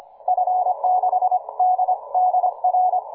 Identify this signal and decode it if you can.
Morse code, heard as a keyed tone, about 26 wpm, 740 Hz. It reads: F6DDR
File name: cw forum / K9DX 1818 le 17-12-2008 05h47